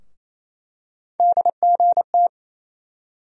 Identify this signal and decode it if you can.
Morse code, heard as a keyed tone, about 28 wpm, 700 Hz. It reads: DGT
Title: USA Call Sign Postfixes - 3 Letters 28wpm